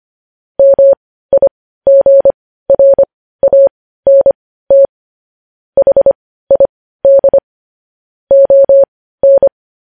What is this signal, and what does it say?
Morse code, heard as a keyed tone, about 25 wpm, spaced as for 15 wpm, 550 Hz. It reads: MIGRANT HID ON
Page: News headlines in Morse code at 15 wpm